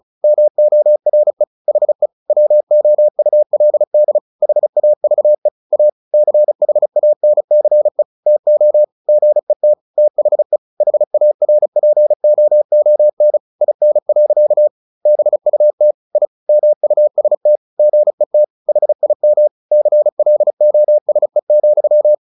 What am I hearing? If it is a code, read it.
Morse code, heard as a keyed tone, about 34 wpm, 600 Hz. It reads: MORE HE WOULD HAVE A CHANCE TO GET THE HARPOON IN. BUT I MUST GET HIM CLOSE,
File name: the old man and the sea 2 35wpm